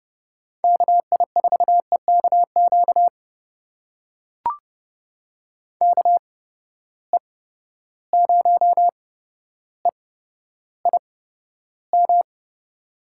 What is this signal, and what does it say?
Morse code, heard as a keyed tone, about 30 wpm, spaced as for 13 wpm, 700 Hz. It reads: KI4EKQ KE0EIM
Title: USA Call Signs - 6 Characters - 8x Character Spacing - 30wpm